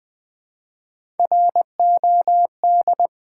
Morse code, heard as a keyed tone, about 20 wpm, 700 Hz. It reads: ROD
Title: Build A Word - 1 Letter At A Time - 3-Letter Words - 20wpm